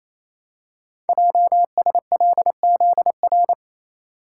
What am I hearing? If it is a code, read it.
Morse code, heard as a keyed tone, about 28 wpm, 700 Hz. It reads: JSLZR
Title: Code Groups - 5 Letters - 28wpm